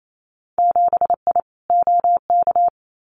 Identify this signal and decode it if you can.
Morse code, heard as a keyed tone, about 28 wpm, 700 Hz. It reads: 7I OK
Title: Winter Field Day Exchanges Encore 28wpm